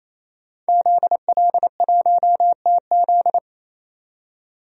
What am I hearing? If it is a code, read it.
Morse code, heard as a keyed tone, about 28 wpm, 700 Hz. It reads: ZL1TZ
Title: DXCC-100 Call Signs - 5 Characters - 28wpm